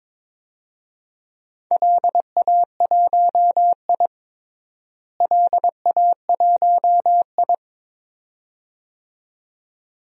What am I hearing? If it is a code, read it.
Morse code, heard as a keyed tone, about 22 wpm, 700 Hz. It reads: LA1I LA1I
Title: DXCC-100 Call Signs - 4 Characters - Repeated 2x - 22wpm